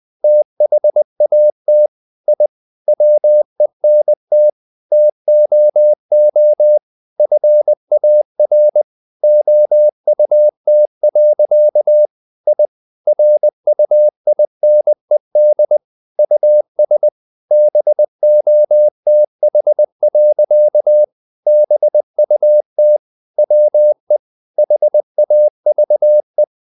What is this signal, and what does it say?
Morse code, heard as a keyed tone, about 20 wpm, 600 Hz. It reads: THAT I WENT TOO FAR OUT. I RUINED US BOTH. BUT WE HAVE
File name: the old man and the sea part 2 20wpm